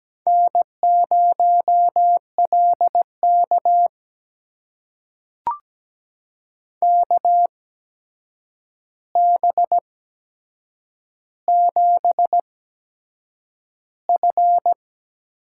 Morse code, heard as a keyed tone, about 17 wpm, spaced as for 7 wpm, 700 Hz. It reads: N0LK KB7F
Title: USA Call Signs - 4 Characters - 8x Character Spacing - 17wpm